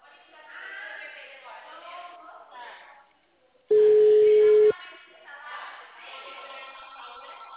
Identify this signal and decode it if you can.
Ringback tone (Europe, 425 Hz), with speech.
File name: out- -2044- - - .51952